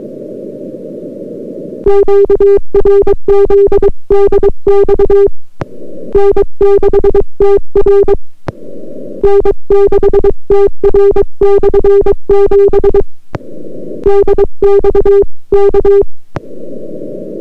Morse code, heard as a keyed tone, about 22 wpm, 400 Hz. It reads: QRZ DX N6TR N6TR/7 DX K